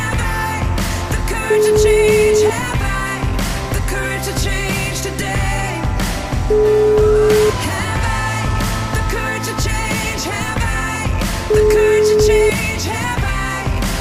Ringback tone (Europe, 425 Hz), with noise.